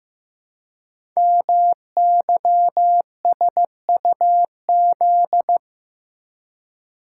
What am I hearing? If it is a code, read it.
Morse code, heard as a keyed tone, about 15 wpm, 700 Hz. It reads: MYSUZ
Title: Code Groups - 5 Letters - 15wpm